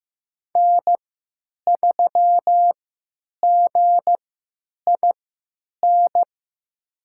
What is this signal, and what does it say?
Morse code, heard as a keyed tone, about 15 wpm, spaced as for 9 wpm, 700 Hz. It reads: N3GIN
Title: USA Call Signs - 5 Characters - 3x Character Spacing - 15wpm